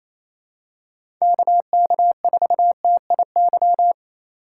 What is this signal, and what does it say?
Morse code, heard as a keyed tone, about 27 wpm, 700 Hz. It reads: KK4TIY
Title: USA Call Signs - 6 Characters - 4x Character Spacing - 28wpm